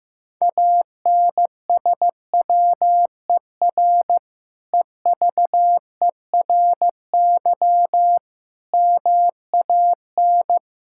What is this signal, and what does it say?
Morse code, heard as a keyed tone, about 15 wpm, 700 Hz. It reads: ANSWER EVERY MAN
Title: Sentences from Top 1000 Words - Review Part 5 - 15wpm